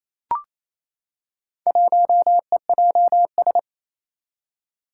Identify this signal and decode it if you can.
Morse code, heard as a keyed tone, about 29 wpm, 700 Hz. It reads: 1EJS
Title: Four Challenging Letter-Number Combinations 28wpm